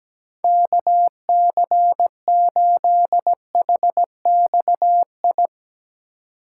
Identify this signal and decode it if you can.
Morse code, heard as a keyed tone, about 17 wpm, 700 Hz. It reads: KC8HXI
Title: USA Call Signs - 6 Characters - 2x Character Spacing - 17wpm